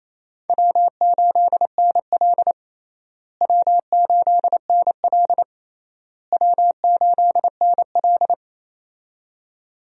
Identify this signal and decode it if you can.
Morse code, heard as a keyed tone, about 28 wpm, 700 Hz. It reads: W8NL W8NL W8NL
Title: USA Call Signs - 4 Characters - Repeated 3x - 28wpm